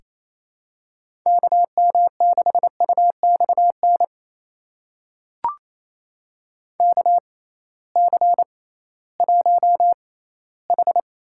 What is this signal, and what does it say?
Morse code, heard as a keyed tone, about 28 wpm, spaced as for 16 wpm, 700 Hz. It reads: KM6UXN KC1H
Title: USA Call Signs - 6x Character Spacing - 28wpm